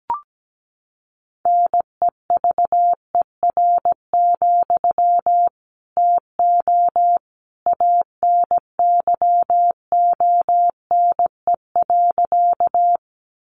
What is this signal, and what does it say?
Morse code, heard as a keyed tone, about 17 wpm, 700 Hz. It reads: NEVER, TO ANYONE.
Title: Sentences from Top 900 Words 17wpm